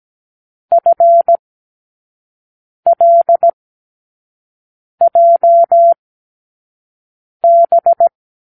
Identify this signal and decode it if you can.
Morse code, heard as a keyed tone, about 17 wpm, spaced as for 5 wpm, 680 Hz. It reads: FLJB
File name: k7qo / Lesson18(A-L)